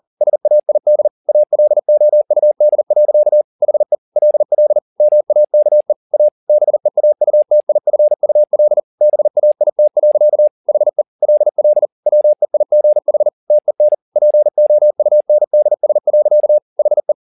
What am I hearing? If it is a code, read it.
Morse code, heard as a keyed tone, about 40 wpm, 600 Hz. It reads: SAID ALOUD. HE LL MAKE A BEAUTIFUL BAIT. HE LL WEIGH TEN POUNDS. HE